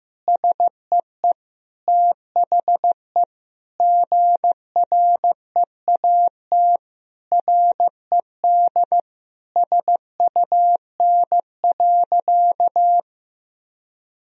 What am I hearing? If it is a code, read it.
Morse code, heard as a keyed tone, about 15 wpm, 700 Hz. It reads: SEE THE GREAT RED SUN.
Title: Sentences from Top 1000 Words - Review Part 2 - 15wpm